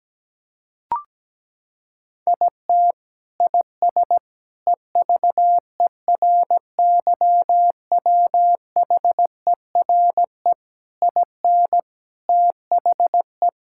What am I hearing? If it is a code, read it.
Morse code, heard as a keyed tone, about 17 wpm, 700 Hz. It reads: IT IS EVERYWHERE IN THE
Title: Sentences from Top 1600 Words 17wpm